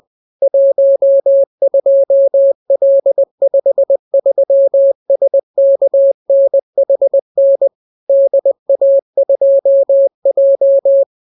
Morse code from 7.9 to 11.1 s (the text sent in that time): DA2J